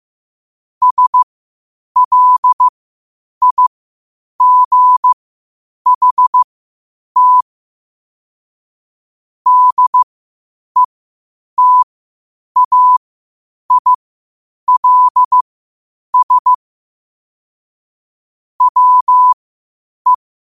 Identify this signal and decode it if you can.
Morse code, heard as a keyed tone, about 15 wpm, spaced as for 8 wpm, 1000 Hz. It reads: SLIGHT DETAILS WE
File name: qotdfiles / 8wpm